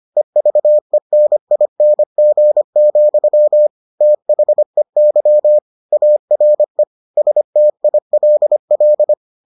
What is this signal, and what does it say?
Morse code, heard as a keyed tone, about 25 wpm, 600 Hz. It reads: EVENING, THEY ARE STILL